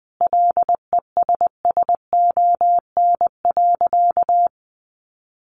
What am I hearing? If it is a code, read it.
Morse code, heard as a keyed tone, about 20 wpm, 700 Hz. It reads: LESSON.
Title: Sentences from Top 1400 Words 20wpm